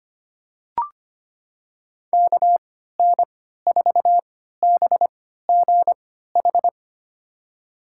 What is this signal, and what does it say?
Morse code, heard as a keyed tone, about 25 wpm, spaced as for 14 wpm, 700 Hz. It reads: KN4BGH
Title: USA Call Signs - 6 Characters - 3x Character Spacing - 25wpm